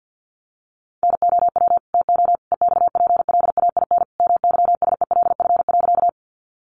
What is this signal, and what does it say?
Morse code, heard as a keyed tone, about 50 wpm, 700 Hz. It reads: NOW TO EXPLAIN MYSELF.
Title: Sentences from Top 1300 Words 50wpm